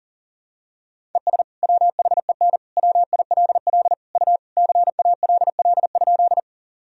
Morse code, heard as a keyed tone, about 40 wpm, 700 Hz. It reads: ES WHEN WILL U CALL?
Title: Sentences from Top 100 Words and Abbreviations 40wpm